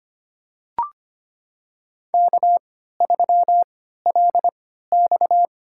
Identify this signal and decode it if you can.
Morse code, heard as a keyed tone, about 25 wpm, spaced as for 14 wpm, 700 Hz. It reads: K3LX
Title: USA Call Signs - 4 Characters - 3x Character Spacing - 25wpm